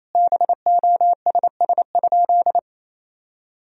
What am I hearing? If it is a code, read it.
Morse code, heard as a keyed tone, about 28 wpm, 700 Hz. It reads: BOSS?
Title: Sentences from Top 1000 Words - Review Part 4 - 28wpm